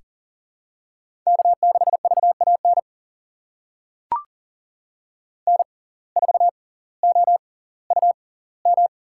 Morse code, heard as a keyed tone, about 40 wpm, spaced as for 20 wpm, 700 Hz. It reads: K6VAN N4OUM
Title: USA Call Signs - 5 Characters - 6x Character Spacing - 40wpm